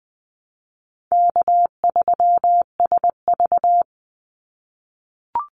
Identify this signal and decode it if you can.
Morse code, heard as a keyed tone, about 19 wpm, 700 Hz. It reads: K3SV
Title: Easy Call Signs - 20wpm